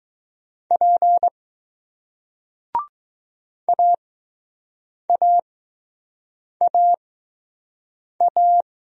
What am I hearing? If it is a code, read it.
Morse code, heard as a keyed tone, about 20 wpm, spaced as for 6 wpm, 700 Hz. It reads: P AAAA